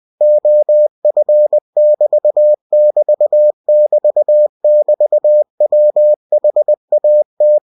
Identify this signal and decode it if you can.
Morse code, heard as a keyed tone, about 20 wpm, 600 Hz. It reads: OF<BT><BT><BT><BT>WHAT
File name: cw-books / winnie-the-pooh / winnie-20wpm-page-0024